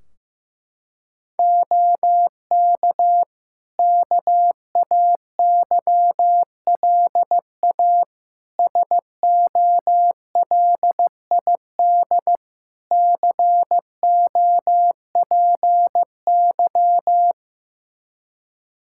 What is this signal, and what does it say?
Morse code, heard as a keyed tone, about 15 wpm, 700 Hz. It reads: OK KAYLA SOLID COPY
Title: K1USN SST QSO Practice 15wpm